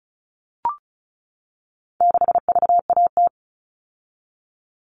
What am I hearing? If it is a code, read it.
Morse code, heard as a keyed tone, about 34 wpm, 700 Hz. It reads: 6VAT
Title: All Cut Numbers - 4-Digit 35wpm